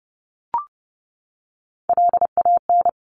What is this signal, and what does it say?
Morse code, heard as a keyed tone, about 30 wpm, 700 Hz. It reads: LAN
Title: Top Three Letter Combinations in Words 30wpm